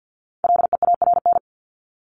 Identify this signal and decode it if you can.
Morse code, heard as a keyed tone, about 50 wpm, 700 Hz. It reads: LEARN
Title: ED Suffix - Top 100 Words with Base Words 50wpm